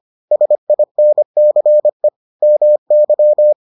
Morse code, heard as a keyed tone, about 25 wpm, 600 Hz. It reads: SINCE MY